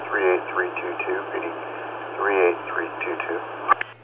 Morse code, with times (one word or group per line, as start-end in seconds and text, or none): none